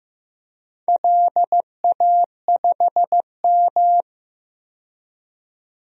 Morse code, heard as a keyed tone, about 15 wpm, 700 Hz. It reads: LA5M
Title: DXCC-100 Call Signs - 4 Characters - 15wpm